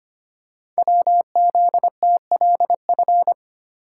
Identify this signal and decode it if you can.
Morse code, heard as a keyed tone, about 25 wpm, 700 Hz. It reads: WZTLF
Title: Code Groups - 5 Letters - 25wpm